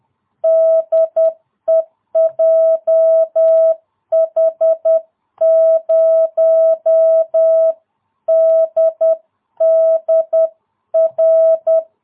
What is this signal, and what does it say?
Morse code, heard as a keyed tone, about 10 wpm, 660 Hz. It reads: DE JH0DDR